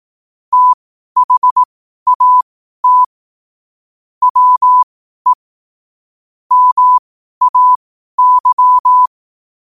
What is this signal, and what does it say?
Morse code, heard as a keyed tone, about 18 wpm, spaced as for 12 wpm, 1000 Hz. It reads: THAT WE MAY
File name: qotdfiles / 12wpm